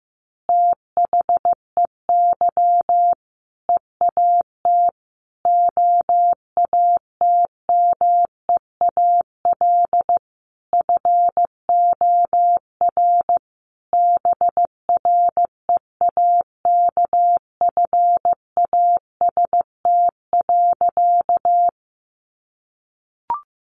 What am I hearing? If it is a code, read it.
Morse code, heard as a keyed tone, about 14 wpm, 700 Hz. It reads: THEY EAT OATMEAL FOR BREAKFAST.
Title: Easy Phrases 15wpm